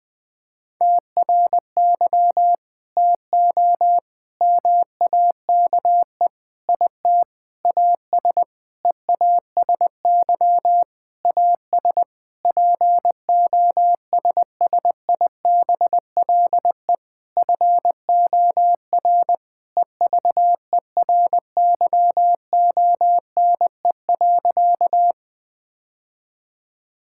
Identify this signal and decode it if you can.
Morse code, heard as a keyed tone, about 21 wpm, 700 Hz. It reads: TRY TO MAKE IT AS EASY AS POSSIBLE FOR EVERYONE.
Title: Sentences with 8-Word Phrases - 20wpm